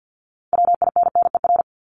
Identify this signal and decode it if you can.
Morse code, heard as a keyed tone, about 50 wpm, 700 Hz. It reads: WINNER
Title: Top 1751-2000 Words 50wpm